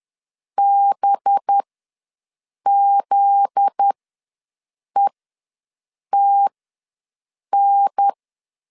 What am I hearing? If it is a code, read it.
Morse code, heard as a keyed tone, about 10 wpm, spaced as for 6 wpm, 800 Hz. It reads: BZETN
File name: LEZIONI / 7 GF9Z / 7 GF9Z ws 10wpm 3919